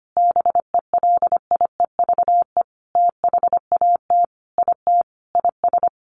Morse code, heard as a keyed tone, about 25 wpm, 700 Hz. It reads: BELIEVE THAT IT IS
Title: Sentences from Top 1500 Words 25wpm